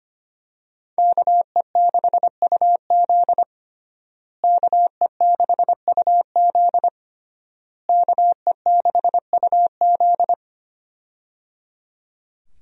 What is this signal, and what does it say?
Morse code, heard as a keyed tone, about 25 wpm, 700 Hz. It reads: KE6UZ KE6UZ KE6UZ
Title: USA Call Signs - 5 Characters - Repeated 3x - 25wpm